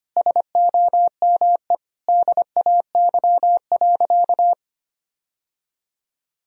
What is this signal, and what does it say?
Morse code, heard as a keyed tone, about 25 wpm, 700 Hz. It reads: SOME DAY.
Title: Sentences from Top 1000 Words - Review Part 4 - 25wpm